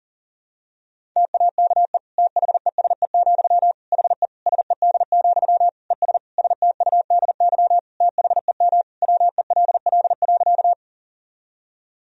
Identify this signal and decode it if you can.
Morse code, heard as a keyed tone, about 40 wpm, 700 Hz. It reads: TAKE THESE, HE SED, ES STUDY THEM WELL.
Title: Sentences from Top 300 Words and Abbreviations 40wpm